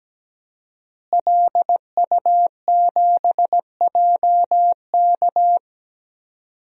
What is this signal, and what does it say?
Morse code, heard as a keyed tone, about 17 wpm, 700 Hz. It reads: LU7JK
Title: DXCC-100 Call Signs - 5 Characters - Repeated 2x - 17wpm